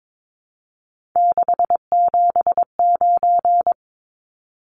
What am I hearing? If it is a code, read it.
Morse code, heard as a keyed tone, about 22 wpm, 700 Hz. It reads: 679
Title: Numbers - 3-Digit 22wpm